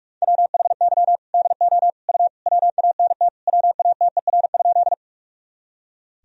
Morse code, heard as a keyed tone, about 42 wpm, 700 Hz. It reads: WHY DO U WANT WATER?
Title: Sentences from Top 200 Words and Abbreviations 45wpm